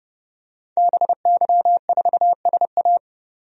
Morse code, heard as a keyed tone, about 32 wpm, 700 Hz. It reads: BY4SA
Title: DXCC-100 Call Signs - 5 Characters - 30wpm